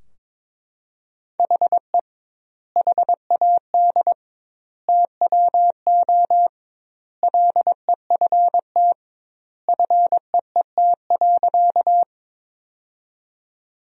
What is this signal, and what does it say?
Morse code, heard as a keyed tone, about 22 wpm, 700 Hz. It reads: HE HAD TWO LEFT FEET.
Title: 2x Word-Spacing - Sentences from Top 300 Words - 22wpm